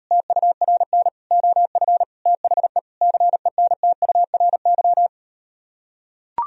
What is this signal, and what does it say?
Morse code, heard as a keyed tone, about 38 wpm, 700 Hz. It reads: TURN OF THE CENTURY